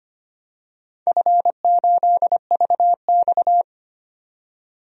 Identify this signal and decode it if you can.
Morse code, heard as a keyed tone, about 25 wpm, 700 Hz. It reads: F8VX
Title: DXCC-100 Call Signs - 4 Characters - 25wpm